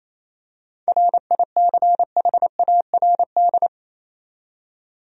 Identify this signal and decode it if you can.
Morse code, heard as a keyed tone, about 28 wpm, 700 Hz. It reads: RICHARD
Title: USA Common Names - 28wpm